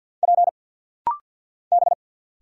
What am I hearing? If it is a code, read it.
Morse code, heard as a keyed tone, about 50 wpm, 700 Hz. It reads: P B
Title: Single Letters - Rapid-Fire 50wpm